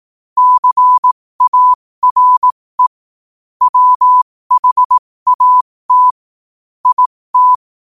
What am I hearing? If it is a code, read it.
Morse code, heard as a keyed tone, about 18 wpm, spaced as for 15 wpm, 1000 Hz. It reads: CARE WHAT IT
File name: qotdfiles / 15wpm